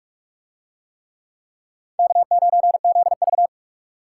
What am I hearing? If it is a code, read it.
Morse code, heard as a keyed tone, about 41 wpm, 700 Hz. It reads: K9ZV